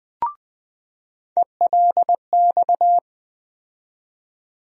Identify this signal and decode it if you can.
Morse code, heard as a keyed tone, about 20 wpm, 700 Hz. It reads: ELX
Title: USA Call Sign Postfixes - 3 Letters 20wpm